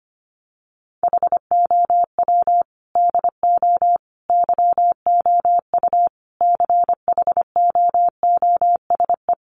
Morse code, heard as a keyed tone, about 25 wpm, 700 Hz. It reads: HOW DO YOU CHOOSE
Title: Sentences from Top 700 Words 25wpm